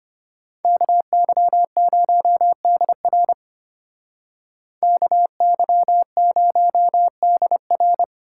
Morse code, heard as a keyed tone, about 31 wpm, 700 Hz. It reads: KY0DR KY0DR
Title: Speed Racing - USA Call Signs 25wpm